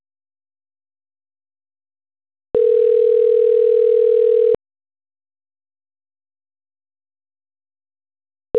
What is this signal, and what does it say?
Ringback tone (North America), 440 + 480 Hz.
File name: autotestagent / 11-25-36.136..callcentertransfer. .456cd544-88ae-42a3-a4fb